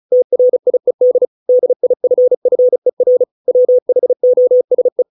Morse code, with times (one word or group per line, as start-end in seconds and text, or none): 0.1-1.3 TRIED
1.5-3.2 DIFFER
3.5-5.0 WHOSE